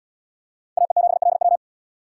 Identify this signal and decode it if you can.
Morse code, heard as a keyed tone, about 70 wpm, 700 Hz. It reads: AE7LW